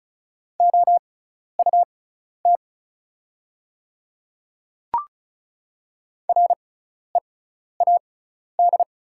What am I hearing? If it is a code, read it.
Morse code, heard as a keyed tone, about 36 wpm, spaced as for 12 wpm, 700 Hz. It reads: OUT READ
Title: 6x Character Spacing - Top 200 Words - 35wpm